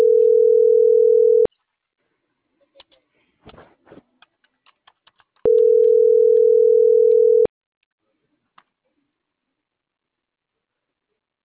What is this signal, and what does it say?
Ringback tone (North America), 440 + 480 Hz.